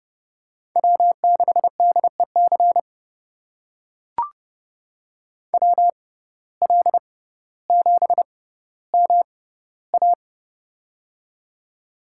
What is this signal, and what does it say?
Morse code, heard as a keyed tone, about 29 wpm, spaced as for 15 wpm, 700 Hz. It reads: W6DEC WL7MA
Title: USA Call Signs - 5 Characters - 6x Character Spacing - 30wpm